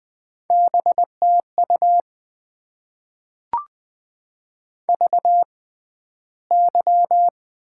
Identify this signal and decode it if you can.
Morse code, heard as a keyed tone, about 20 wpm, spaced as for 12 wpm, 700 Hz. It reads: BTU VY
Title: Top 90 QSO Elements - 6x Character Spacing - 20wpm